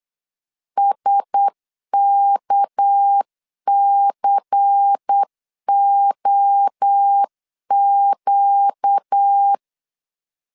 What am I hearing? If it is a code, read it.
Morse code, heard as a keyed tone, about 8 wpm, 800 Hz. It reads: SKCOQ
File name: LEZIONI / 7 GF9Z / 7 GF 8wpm 3108